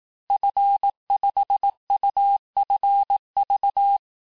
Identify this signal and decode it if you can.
Morse code, heard as a keyed tone, about 18 wpm, 800 Hz. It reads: F5UFV
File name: cours-du-lundi files / cours-1852-FIN-du-QSO-vitesse-17MPM